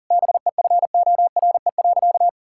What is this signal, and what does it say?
Morse code, heard as a keyed tone, about 40 wpm, 700 Hz. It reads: BEFORE.
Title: Sentences from Top 1000 Words - Review Part 5 - 40wpm